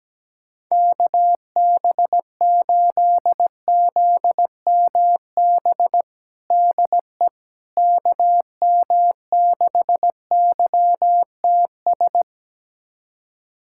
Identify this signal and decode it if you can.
Morse code, heard as a keyed tone, about 17 wpm, 700 Hz. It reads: KB8ZMB DE KM6YTS
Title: USA Callsign DE Callsign - 6 Characters - 17wpm